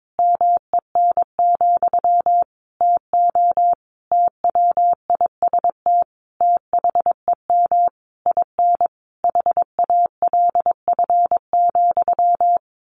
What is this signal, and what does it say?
Morse code, heard as a keyed tone, about 22 wpm, 700 Hz. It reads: MEN, TO TWIST THEM IN HALF,
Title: A Sound Of Thunder 22wpm